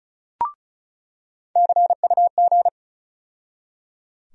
Morse code, heard as a keyed tone, about 34 wpm, 700 Hz. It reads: CUG
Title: USA Call Sign Postfixes - 3 Letters 35wpm